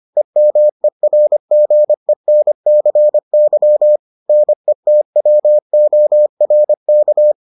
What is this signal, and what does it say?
Morse code, heard as a keyed tone, about 25 wpm, 600 Hz. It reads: EMERGENCY NETWORK